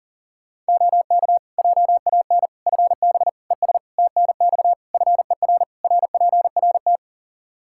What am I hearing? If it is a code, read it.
Morse code, heard as a keyed tone, about 40 wpm, 700 Hz. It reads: OK JAN FB ES TNX FER RPRT
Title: K1USN SST QSO Practice 40wpm